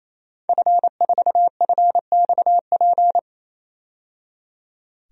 Morse code, heard as a keyed tone, about 28 wpm, 700 Hz. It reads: F4FXP
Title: DXCC-100 Call Signs - 5 Characters - 28wpm